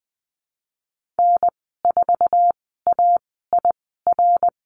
Morse code, heard as a keyed tone, about 20 wpm, spaced as for 15 wpm, 700 Hz. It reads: N4AIR
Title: 062 Lesson 11g - Letter h - Call Signs 20wpm